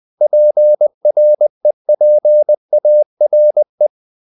Morse code, heard as a keyed tone, about 20 wpm, 600 Hz. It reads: PREPARE
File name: the old man and the sea part 2 20wpm